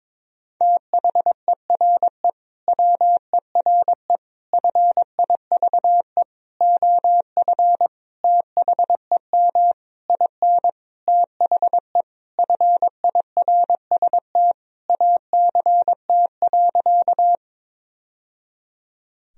Morse code, heard as a keyed tone, about 22 wpm, 700 Hz. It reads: THERE WERE FIVE OF THEM IN THE FIRST ACT.
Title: Sentences from Top 1000 Words - Review Part 2 - 22wpm